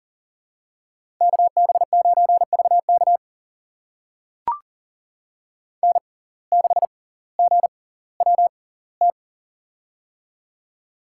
Morse code, heard as a keyed tone, about 40 wpm, spaced as for 20 wpm, 700 Hz. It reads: KB9VK N6GWT
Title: USA Call Signs - 5 Characters - 6x Character Spacing - 40wpm